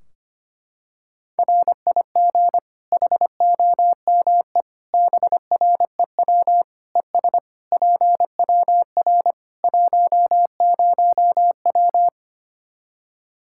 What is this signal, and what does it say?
Morse code, heard as a keyed tone, about 25 wpm, 700 Hz. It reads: RIG HOME BREW ES PWR 10W
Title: Practice QSO - Exchange 2 - Info - 25wpm